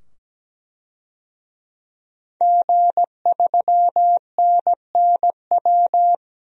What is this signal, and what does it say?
Morse code, heard as a keyed tone, about 17 wpm, 700 Hz. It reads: G3NNW
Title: DXCC-100 Call Signs - 5 Characters - 17wpm